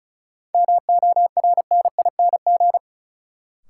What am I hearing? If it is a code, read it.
Morse code, heard as a keyed tone, about 36 wpm, 700 Hz. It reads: MORNING
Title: Sentences from Top 400 Words - New Word Drill - 35wpm